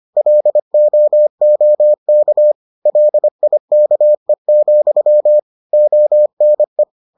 Morse code, heard as a keyed tone, about 25 wpm, 600 Hz. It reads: LOOK LIKE, ONE